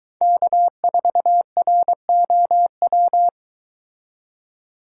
Morse code, heard as a keyed tone, about 22 wpm, 700 Hz. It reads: K4ROW